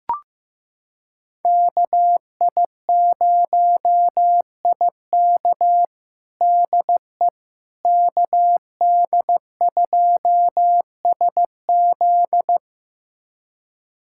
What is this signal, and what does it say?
Morse code, heard as a keyed tone, about 15 wpm, 700 Hz. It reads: KI0IK DE KD2SZ
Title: USA Callsign DE Callsign - 5 Characters - 15wpm